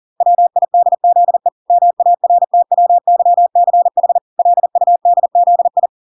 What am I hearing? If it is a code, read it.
Morse code, heard as a keyed tone, about 40 wpm, 700 Hz. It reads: WIDZE MARTWYCH LUDZI